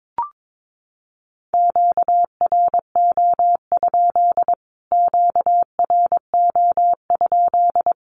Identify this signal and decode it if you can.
Morse code, heard as a keyed tone, about 22 wpm, 700 Hz. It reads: QRO? QRO?
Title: Q-Codes Repeated 3x 22wpm